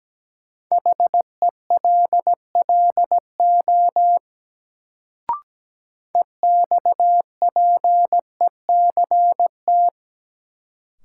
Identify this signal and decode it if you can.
Morse code, heard as a keyed tone, about 17 wpm, 700 Hz. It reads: HELLO EXPECT
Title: Sentences from Top 1100 Words - New Word Drill Alt - 17wpm